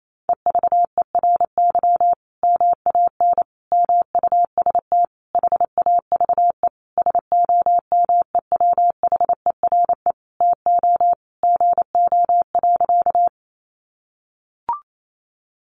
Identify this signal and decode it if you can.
Morse code, heard as a keyed tone, about 28 wpm, 700 Hz. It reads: EVERY MAN MUST HAVE SOMEWHERE TO GO.
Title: Sentences from Top 1400 Words 28wpm